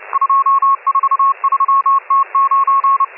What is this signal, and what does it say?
Morse code, heard as a keyed tone, about 30 wpm, 1090 Hz, with radio noise. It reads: 243T9